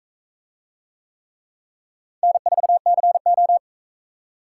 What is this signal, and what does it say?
Morse code, heard as a keyed tone, about 42 wpm, 700 Hz. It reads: N4CO